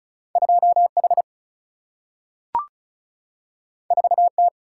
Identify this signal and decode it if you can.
Morse code, heard as a keyed tone, about 36 wpm, 700 Hz. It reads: 2 H 4 T
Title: Single-Letter Single-Number 35wpm